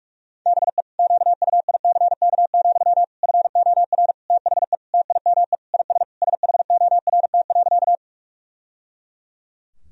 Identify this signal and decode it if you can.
Morse code, heard as a keyed tone, about 46 wpm, 700 Hz. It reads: BE QUICK, FOR THE TIME IS SHORT.
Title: Sentences from Top 1000 Words - Review Part 2 - 45wpm